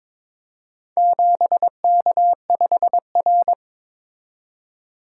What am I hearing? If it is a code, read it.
Morse code, heard as a keyed tone, about 22 wpm, 700 Hz. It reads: 7K5R
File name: practice / 22wpm / Four Letter-Number Combinations - 22wpm